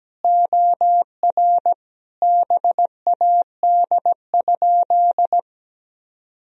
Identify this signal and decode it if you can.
Morse code, heard as a keyed tone, about 17 wpm, 700 Hz. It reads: OR BAD?
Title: Sentences from Top 1000 Words - Review Part 1 - 17wpm